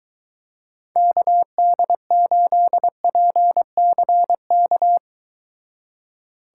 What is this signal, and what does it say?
Morse code, heard as a keyed tone, about 23 wpm, 700 Hz. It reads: KD8PCK